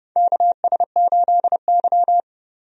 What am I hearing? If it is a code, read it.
Morse code, heard as a keyed tone, about 30 wpm, 700 Hz. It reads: KS8Y
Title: USA Call Signs - 4 Characters - 30wpm